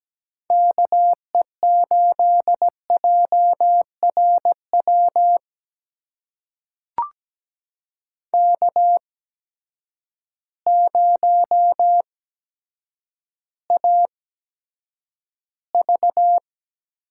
Morse code, heard as a keyed tone, about 17 wpm, spaced as for 8 wpm, 700 Hz. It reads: KE8JRW K0AV
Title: USA Call Signs - 8x Character Spacing - 17wpm